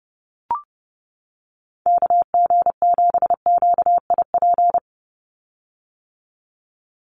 Morse code, heard as a keyed tone, about 30 wpm, 700 Hz. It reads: KG7QIP
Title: USA Call Signs 30wpm